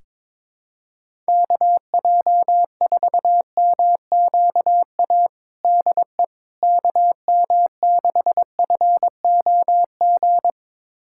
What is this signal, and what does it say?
Morse code, heard as a keyed tone, about 22 wpm, 700 Hz. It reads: KJ4MQA DE KM6FOG
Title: USA Callsign DE Callsign - 6 Characters - 22wpm